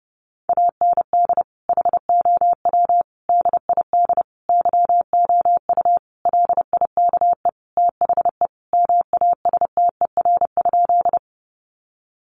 Morse code, heard as a keyed tone, about 30 wpm, 700 Hz. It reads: AND HOW DID YOU LIKE THE MASTER?
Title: Sentences from Top 1300 Words 30wpm